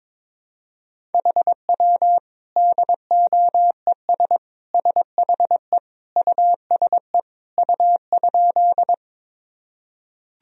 Morse code, heard as a keyed tone, about 22 wpm, 700 Hz. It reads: HW DOES SHE USE U?
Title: Sentences from Top 200 Words and Abbreviations 22wpm